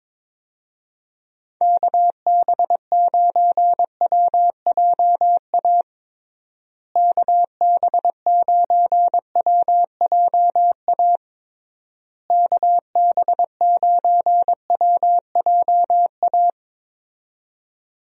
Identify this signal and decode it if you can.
Morse code, heard as a keyed tone, about 22 wpm, 700 Hz. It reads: KB9WJA KB9WJA KB9WJA
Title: USA Call Signs - 6 Characters - Repeated 3x - 22wpm